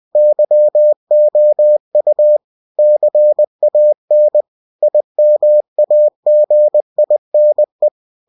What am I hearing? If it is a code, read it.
Morse code, heard as a keyed tone, about 20 wpm, 600 Hz. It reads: YOU CAN IMAGINE
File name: cw-books / winnie-the-pooh / winnie-20wpm-page-0100